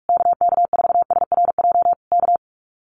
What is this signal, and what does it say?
Morse code, heard as a keyed tone, about 44 wpm, 700 Hz. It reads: KK4SRJ K
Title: Common QSO Phrases Encore 45wpm